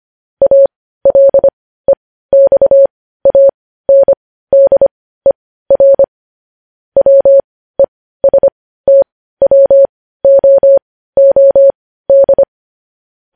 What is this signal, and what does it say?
Morse code, heard as a keyed tone, about 25 wpm, spaced as for 15 wpm, 550 Hz. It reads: ALEXANDER WESTWOOD
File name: podcast / News headlines-1x-25x15